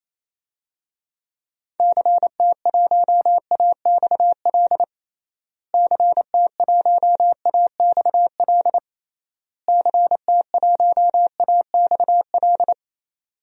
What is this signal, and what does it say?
Morse code, heard as a keyed tone, about 28 wpm, 700 Hz. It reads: CT1AXL CT1AXL CT1AXL
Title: DXCC-100 Call Signs - 6 Characters - Repeated 3x - 28wpm